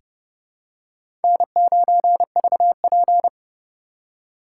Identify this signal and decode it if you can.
Morse code, heard as a keyed tone, about 32 wpm, 700 Hz. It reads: N9VP
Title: USA Call Signs - 4 Characters - 3x Character Spacing - 30wpm